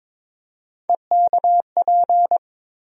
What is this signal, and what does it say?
Morse code, heard as a keyed tone, about 22 wpm, 700 Hz. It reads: EKP
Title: USA Call Sign Postfixes - 3 Letters 22wpm